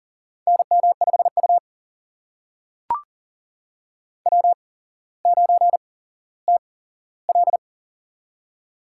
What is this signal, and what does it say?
Morse code, heard as a keyed tone, about 39 wpm, spaced as for 17 wpm, 700 Hz. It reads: NM5U W9TL
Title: USA Call Signs - 4 Characters - 8x Character Spacing - 40wpm